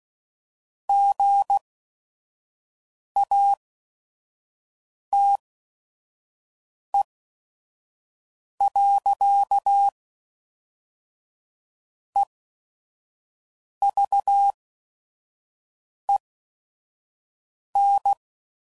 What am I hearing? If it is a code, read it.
Morse code, heard as a keyed tone, about 16 wpm, spaced as for 5 wpm, 800 Hz. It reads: GATE. EVEN
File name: CW Code Groups @5WPM / Technical Subjects / AND Gate